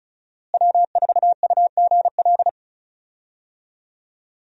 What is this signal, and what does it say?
Morse code, heard as a keyed tone, about 36 wpm, 700 Hz. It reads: W4UGL
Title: USA Call Signs - 5 Characters - Repeated 3x - 35wpm